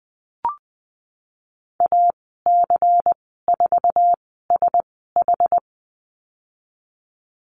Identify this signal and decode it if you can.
Morse code, heard as a keyed tone, about 20 wpm, spaced as for 15 wpm, 700 Hz. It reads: AC4SH
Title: 193 Lesson 31g - Letter v - Call Signs 20wpm